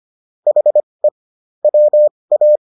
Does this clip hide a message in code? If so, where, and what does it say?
Yes: Morse code from 0.5 to 2.6 s, heard as a keyed tone, about 25 wpm, spaced as for 20 wpm, 600 Hz. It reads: HE WA